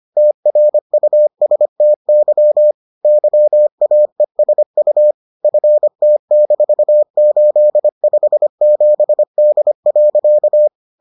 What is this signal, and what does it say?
Morse code, heard as a keyed tone, about 25 wpm, 600 Hz. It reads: TRUSTY YAESU FT-857D.